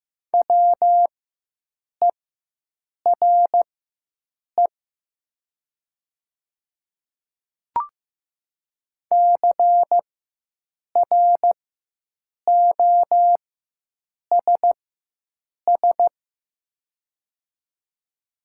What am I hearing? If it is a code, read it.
Morse code, heard as a keyed tone, about 15 wpm, spaced as for 7 wpm, 700 Hz. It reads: WERE CROSS
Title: 4x Character Spacing - Top 300 Words - 15wpm